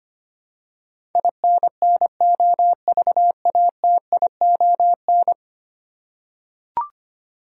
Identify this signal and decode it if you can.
Morse code, heard as a keyed tone, about 25 wpm, 700 Hz. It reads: INNOVATION
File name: Ten Letter Words - 25wpm